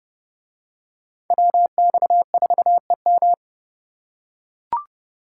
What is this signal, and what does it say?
Morse code, heard as a keyed tone, about 29 wpm, 700 Hz. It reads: WX4EM